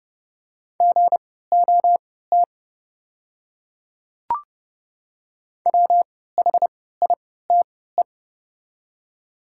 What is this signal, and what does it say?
Morse code, heard as a keyed tone, about 30 wpm, spaced as for 17 wpm, 690 Hz. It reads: GOT WHITE
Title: 3x Character Spacing - Top 300 Words - 30wpm